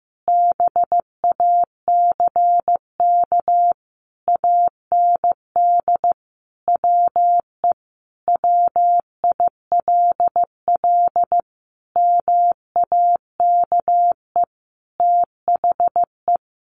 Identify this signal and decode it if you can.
Morse code, heard as a keyed tone, about 15 wpm, 700 Hz. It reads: BACK AND WE WILL MAKE THE
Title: Sentences from Top 1000 Words 15wpm